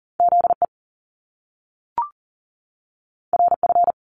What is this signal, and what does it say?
Morse code, heard as a keyed tone, about 41 wpm, 700 Hz. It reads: ZE RF
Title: Top Two Letter Combinations in Words 40wpm